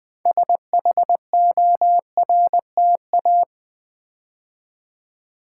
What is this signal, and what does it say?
Morse code, heard as a keyed tone, about 20 wpm, 690 Hz. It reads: SHORTA
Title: Build A Word - 2 Letters At A Time - 8-Letter Words - 20wpm